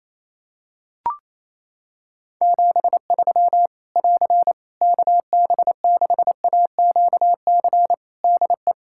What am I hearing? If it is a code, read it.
Morse code, heard as a keyed tone, about 28 wpm, 700 Hz. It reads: 73 <AR> KB6AQC DE
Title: Practice QSO - Exchange 4 - Ending - 28wpm